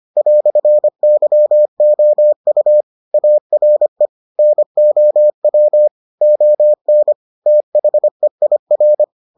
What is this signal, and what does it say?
Morse code, heard as a keyed tone, about 25 wpm, 600 Hz. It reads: "YOU ARE NOW ON THEIR